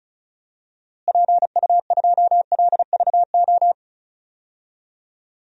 Morse code, heard as a keyed tone, about 34 wpm, 700 Hz. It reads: PU2LVO
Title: DXCC-100 Call Signs - 6 Characters - Repeated 3x - 35wpm